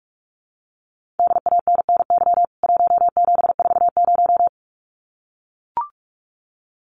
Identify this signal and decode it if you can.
Morse code, heard as a keyed tone, about 44 wpm, 700 Hz. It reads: DANNY 1740
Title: CWT Exchange 45wpm